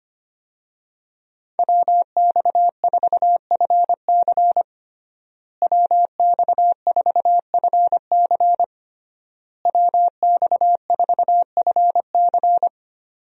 Morse code, heard as a keyed tone, about 25 wpm, 700 Hz. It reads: WX4FC WX4FC WX4FC
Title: USA Call Signs - 5 Characters - Repeated 3x - 25wpm